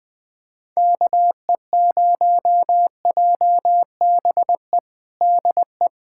Morse code, heard as a keyed tone, about 20 wpm, 700 Hz. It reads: KE0JBE DE
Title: USA Callsign DE Callsign 20wpm